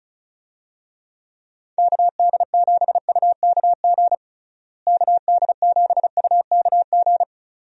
Morse code, heard as a keyed tone, about 36 wpm, 700 Hz. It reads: KD7UKG KD7UKG
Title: USA Call Signs - 6 Characters - Repeated 2x - 35wpm